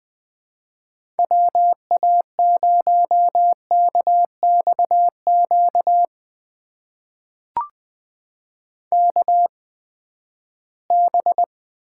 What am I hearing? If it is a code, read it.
Morse code, heard as a keyed tone, about 20 wpm, spaced as for 14 wpm, 700 Hz. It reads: WA0KXQ KB
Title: USA Call Signs - 6 Characters - 8x Character Spacing - 20wpm